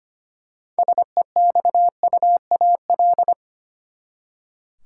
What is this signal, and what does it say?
Morse code, heard as a keyed tone, about 25 wpm, 700 Hz. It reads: SEXUAL
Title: Six Letter Words - 25wpm